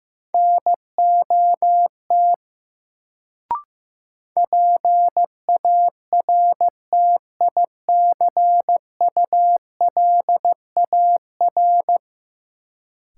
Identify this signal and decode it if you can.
Morse code, heard as a keyed tone, about 15 wpm, 700 Hz. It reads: NOT PARTICULAR
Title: Sentences from Top 600 Words - New Word Drill - 15wpm